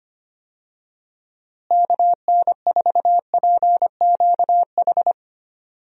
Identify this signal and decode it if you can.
Morse code, heard as a keyed tone, about 25 wpm, 700 Hz. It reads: KN4PQH
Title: USA Call Signs - 6 Characters - Repeated 2x - 25wpm